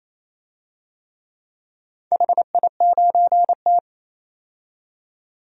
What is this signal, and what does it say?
Morse code, heard as a keyed tone, about 28 wpm, 700 Hz. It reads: HI9T
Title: DXCC-100 Call Signs - 4 Characters - 28wpm